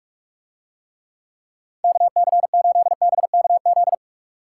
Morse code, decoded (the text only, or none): KC8BKZ